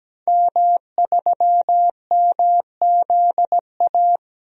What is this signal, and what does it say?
Morse code, heard as a keyed tone, about 17 wpm, 700 Hz. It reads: M3MZA
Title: DXCC-100 Call Signs - 5 Characters - 17wpm